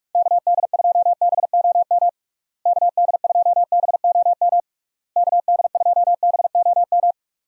Morse code, heard as a keyed tone, about 46 wpm, 700 Hz. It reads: KD2BOM KD2BOM KD2BOM
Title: USA Call Signs - 6 Characters - Repeated 3x - 45wpm